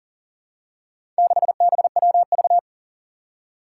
Morse code, heard as a keyed tone, about 40 wpm, 700 Hz. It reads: 6BWV
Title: Four Challenging Letter-Number Combinations 40wpm